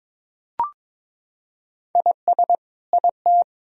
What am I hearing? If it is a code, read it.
Morse code, heard as a keyed tone, about 22 wpm, 700 Hz. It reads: IS IT
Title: Sentences from Top 1000 Words - Review Part 4 - 22wpm